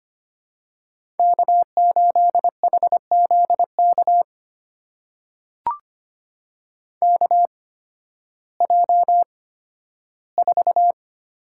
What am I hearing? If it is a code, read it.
Morse code, heard as a keyed tone, about 25 wpm, spaced as for 13 wpm, 700 Hz. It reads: K8HZK KJ4
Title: USA Call Signs - 8x Character Spacing - 25wpm